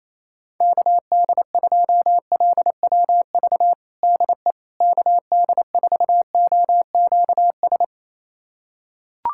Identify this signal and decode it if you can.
Morse code, heard as a keyed tone, about 28 wpm, 700 Hz. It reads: KD2LWV DE KD4OQS
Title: USA Callsign DE Callsign - 6 Characters - 28wpm